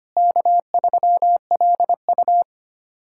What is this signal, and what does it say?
Morse code, heard as a keyed tone, about 25 wpm, 700 Hz. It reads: K3LU
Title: K1USN SST QSO Practice - 25wpm